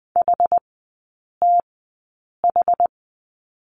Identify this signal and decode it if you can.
Morse code, heard as a keyed tone, about 20 wpm, spaced as for 8 wpm, 700 Hz. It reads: HTH
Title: 100 Lesson 17e - Letter u - Three Characters 20wpm